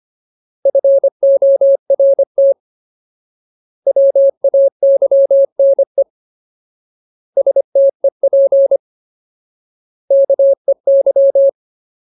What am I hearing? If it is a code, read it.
Morse code, heard as a keyed tone, about 25 wpm, 550 Hz. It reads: FORT WAYNE STEP KEY
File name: Ultimate Word List / 25WPM Ultimate